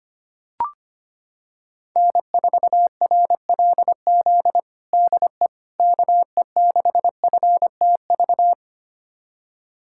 Morse code, heard as a keyed tone, about 25 wpm, 700 Hz. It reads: N4RLZ DE KE6FTV
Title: USA Callsign DE Callsign 25wpm